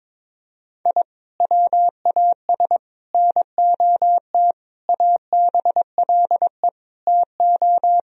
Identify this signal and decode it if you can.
Morse code, heard as a keyed tone, about 22 wpm, 700 Hz. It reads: I WAS NOT ABLE TO
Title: Sentences from Top 1000 Words - Review Part 4 - 22wpm